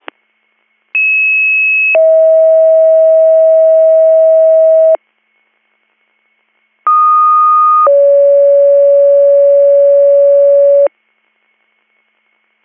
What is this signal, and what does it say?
Two-tone page, heard as a one-second tone, then a three-second tone, about 2600 then 640 Hz, about 1230 then 560 Hz. It is heard from 0.9 to 10.9 s.